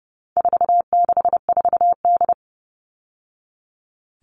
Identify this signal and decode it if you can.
Morse code, heard as a keyed tone, about 30 wpm, 700 Hz. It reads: 464D
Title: All Cut Numbers - 4-Digit 30wpm